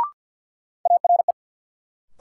Morse code, heard as a keyed tone, about 47 wpm, 700 Hz. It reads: ARE
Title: Sentences from Top 100 Words - New Word Drill - 50wpm